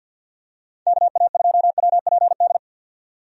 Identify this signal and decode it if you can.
Morse code, heard as a keyed tone, about 52 wpm, 700 Hz. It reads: KA2WPD
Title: USA Call Signs - 6 Characters - 50wpm